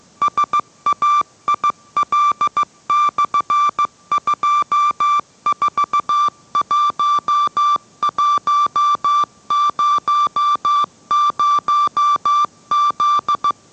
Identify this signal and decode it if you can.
Morse code, heard as a keyed tone, about 15 wpm, 1170 Hz. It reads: SAIL/241100Z